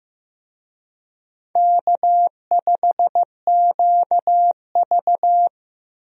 Morse code, heard as a keyed tone, about 15 wpm, 700 Hz. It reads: K5QV
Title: USA Call Signs - 4 Characters - 15wpm